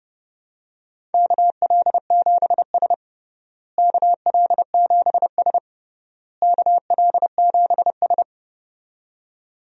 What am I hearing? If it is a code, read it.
Morse code, heard as a keyed tone, about 31 wpm, 700 Hz. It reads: KL7S KL7S KL7S
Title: DXCC-100 Call Signs - 4 Characters - Repeated 3x - 30wpm